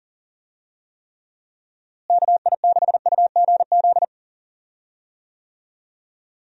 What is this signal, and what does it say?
Morse code, heard as a keyed tone, about 40 wpm, 700 Hz. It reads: KI6UGZ
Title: USA Call Signs - 6 Characters - Repeated 2x - 40wpm